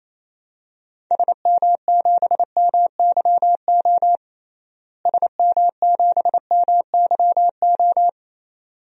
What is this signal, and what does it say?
Morse code, heard as a keyed tone, about 28 wpm, 700 Hz. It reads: SM7MYO SM7MYO
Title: DXCC-100 Call Signs - 6 Characters - Repeated 2x - 28wpm